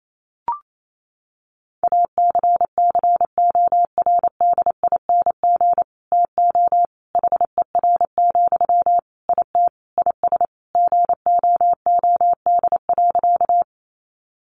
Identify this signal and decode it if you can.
Morse code, heard as a keyed tone, about 28 wpm, 700 Hz. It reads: ACCORDING TO HER, IT IS GOOD.
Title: Sentences from Top 800 Words 28wpm